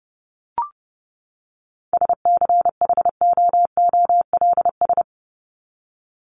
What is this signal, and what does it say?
Morse code, heard as a keyed tone, about 30 wpm, 700 Hz. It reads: SCHOOLS
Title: ES:S Suffix - Top 100 Words with Base Words 30wpm